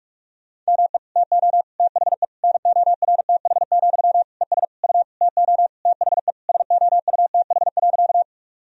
Morse code, heard as a keyed tone, about 46 wpm, 700 Hz. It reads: ME TO THE NORTH, ES U TO THE SOUTH.
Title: Sentences from Top 400 Words and Abbreviations 45wpm